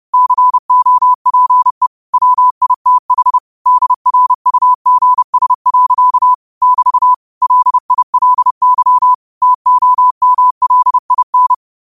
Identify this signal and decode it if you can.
Morse code, heard as a keyed tone, about 30 wpm, 1000 Hz. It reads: COPE WITH DRUGS. <BT> LILY TOMLIN
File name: qotdfiles / 30wpm